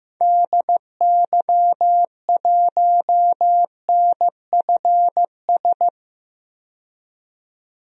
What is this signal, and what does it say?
Morse code, heard as a keyed tone, about 15 wpm, 690 Hz. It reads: DY1NFS
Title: DXCC-100 Call Signs - 6 Characters - 15wpm